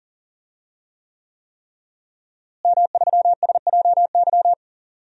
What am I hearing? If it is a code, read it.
Morse code, heard as a keyed tone, about 40 wpm, 700 Hz. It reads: M3SJY
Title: DXCC-100 Call Signs - 5 Characters - Repeated 3x - 40wpm